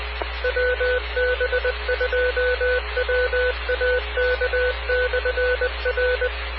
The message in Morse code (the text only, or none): WB2WAK/R